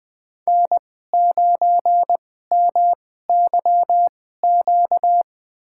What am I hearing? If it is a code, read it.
Morse code, heard as a keyed tone, about 21 wpm, spaced as for 15 wpm, 700 Hz. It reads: N9MYQ
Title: USA Call Signs - 5 Characters - 2x Character Spacing - 20wpm